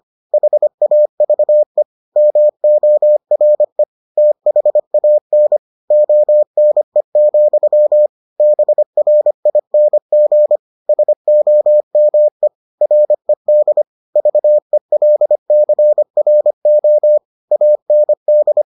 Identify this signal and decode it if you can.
Morse code, heard as a keyed tone, about 25 wpm, 600 Hz. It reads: HAVE MORE THAN ONE, BRING SOME RED VELCRO AND